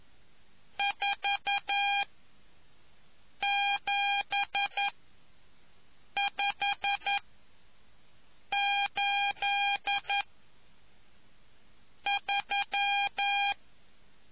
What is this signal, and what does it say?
Morse code, heard as a keyed tone, about 11 wpm, spaced as for 5 wpm, 800 Hz. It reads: 47583